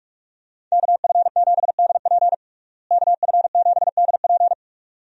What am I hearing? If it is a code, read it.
Morse code, heard as a keyed tone, about 42 wpm, 700 Hz. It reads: KF7DP KF7DP
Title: USA Call Signs - 5 Characters - Repeated 2x - 45wpm